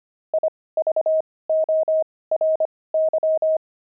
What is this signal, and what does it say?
Morse code, heard as a keyed tone, about 25 wpm, spaced as for 18 wpm, 630 Hz. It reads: IVORY
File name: pangram-fs-18-cs-25-buckles